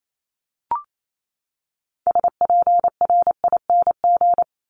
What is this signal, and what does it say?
Morse code, heard as a keyed tone, about 28 wpm, 700 Hz. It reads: SPRING
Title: Top 1001-1250 Words 28wpm